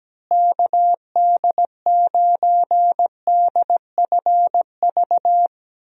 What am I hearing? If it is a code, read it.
Morse code, heard as a keyed tone, about 17 wpm, 700 Hz. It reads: KD9DFV
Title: USA Call Signs - 6 Characters - Repeated 3x - 17wpm